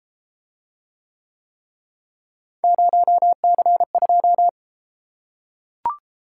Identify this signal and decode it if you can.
Morse code, heard as a keyed tone, about 33 wpm, 700 Hz. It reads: 0C2